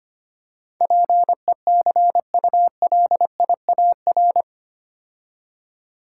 Morse code, heard as a keyed tone, about 25 wpm, 700 Hz. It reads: PECULIAR
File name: 8 Letter Words - First 3 Letters Given - 25wpm